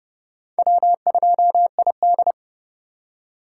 Morse code, heard as a keyed tone, about 32 wpm, 700 Hz. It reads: W2ID
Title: K1USN SST QSO Practice - 30wpm